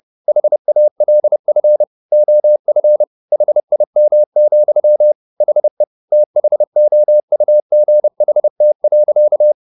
Morse code, heard as a keyed tone, about 30 wpm, 600 Hz. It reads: HALF OF HIM, HE THOUGHT.